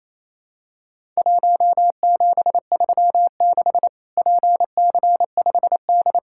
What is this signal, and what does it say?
Morse code, heard as a keyed tone, about 28 wpm, 700 Hz. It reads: 1736 PC5D
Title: Worked All Europe DX Contest - QTC Exchange - 28wpm